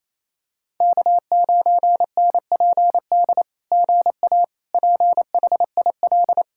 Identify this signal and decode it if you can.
Morse code, heard as a keyed tone, about 28 wpm, 690 Hz. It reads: K9NPD GA PHIL
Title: SOTA QSO Practice 28wpm